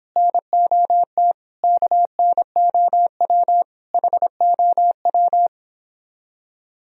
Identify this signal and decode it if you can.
Morse code, heard as a keyed tone, about 25 wpm, 700 Hz. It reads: NOT KNOW HOW